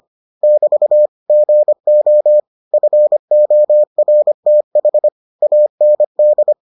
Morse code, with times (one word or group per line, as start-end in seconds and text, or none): 0.4-1.1 <BT>
1.3-2.4 GO
2.7-5.1 FORTH
5.4-6.5 AND